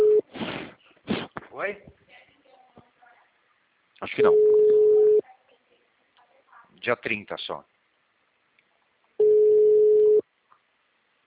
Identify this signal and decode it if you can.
Ringback tone (Europe, 425 Hz), with speech.